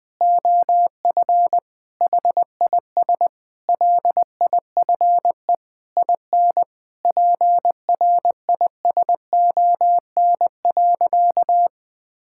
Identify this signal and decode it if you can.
Morse code, heard as a keyed tone, about 21 wpm, 700 Hz. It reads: OF HIS LIFE IN PRISON.
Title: Sentences with 8-Word Phrases - 20wpm